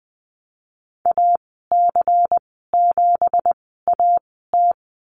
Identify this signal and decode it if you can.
Morse code, heard as a keyed tone, about 20 wpm, spaced as for 14 wpm, 700 Hz. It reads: AC7AT
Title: 173 Lesson 28g - Number 9 - Call Signs 20wpm